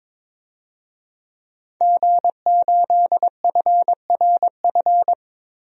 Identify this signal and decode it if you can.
Morse code, heard as a keyed tone, about 22 wpm, 700 Hz. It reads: G8FRF
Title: DXCC-100 Call Signs - 5 Characters - 22wpm